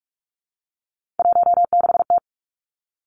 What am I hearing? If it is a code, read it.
Morse code, heard as a keyed tone, about 46 wpm, 700 Hz. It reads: 16T
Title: Three Challenging Letter-Number Combinations 45wpm